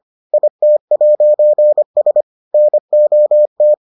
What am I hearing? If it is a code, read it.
Morse code, heard as a keyed tone, about 25 wpm, 600 Hz. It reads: IT'S NOT